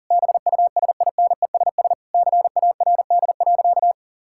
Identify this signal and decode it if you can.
Morse code, heard as a keyed tone, about 37 wpm, 700 Hz. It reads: BUSINESS CARD.
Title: Sentences from Top 1000 Words - Review Part 6 - 40wpm